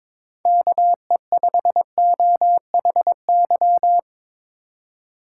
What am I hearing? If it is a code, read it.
Morse code, heard as a keyed tone, about 23 wpm, 700 Hz. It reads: KE5OHY
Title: USA Call Signs - 6 Characters - 2x Character Spacing - 22wpm